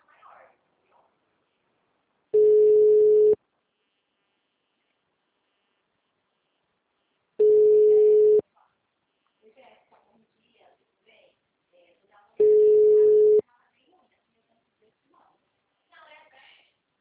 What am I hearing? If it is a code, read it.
Ringback tone (Europe, 425 Hz), with speech.
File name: out- -2001- - - .9661